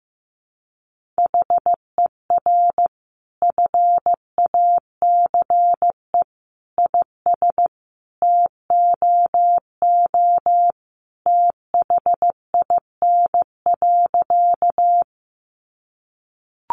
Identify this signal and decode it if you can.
Morse code, heard as a keyed tone, about 15 wpm, 700 Hz. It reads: HER FACE IS TOO THIN.
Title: Sentences from Top 1200 Words 15wpm